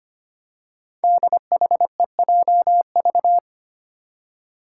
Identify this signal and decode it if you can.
Morse code, heard as a keyed tone, about 25 wpm, 700 Hz. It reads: DHEJV
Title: Five Challenging Letter-Number Combinations 25wpm